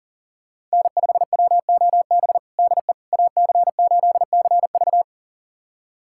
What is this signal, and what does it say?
Morse code, heard as a keyed tone, about 40 wpm, 700 Hz. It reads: N5WOB DE AC8CV
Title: USA Callsign DE Callsign - 5 Characters - 40wpm